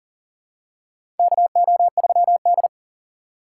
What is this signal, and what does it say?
Morse code, heard as a keyed tone, about 40 wpm, 700 Hz. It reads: KO3D
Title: USA Call Signs - 4 Characters - 3x Character Spacing - 40wpm